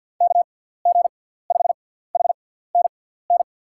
Morse code, heard as a keyed tone, about 51 wpm, spaced as for 17 wpm, 700 Hz. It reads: KG5HNN
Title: USA Call Signs - 6 Characters - 6x Character Spacing - 50wpm